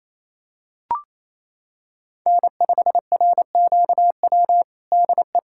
Morse code, heard as a keyed tone, about 28 wpm, 700 Hz. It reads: N5RQW DE
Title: USA Callsign DE Callsign 28wpm